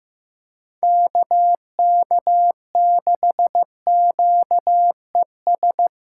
Morse code, heard as a keyed tone, about 16 wpm, 700 Hz. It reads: KK6QES
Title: USA Callsign DE Callsign - 6 Characters - 15wpm